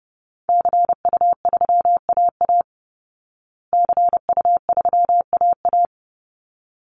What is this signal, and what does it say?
Morse code, heard as a keyed tone, about 30 wpm, 700 Hz. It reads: CU3AA CU3AA
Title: Call Signs Repeated 3x - 30wpm